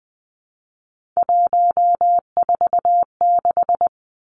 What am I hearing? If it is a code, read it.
Morse code, heard as a keyed tone, about 20 wpm, 700 Hz. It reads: 146
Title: Three Challenging Letter-Number Combinations 20wpm